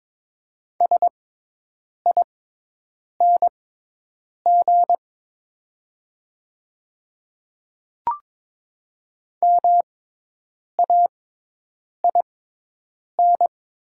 Morse code, heard as a keyed tone, about 22 wpm, spaced as for 8 wpm, 700 Hz. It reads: SING MAIN
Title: 6x Character Spacing - Top 500 Words - 22wpm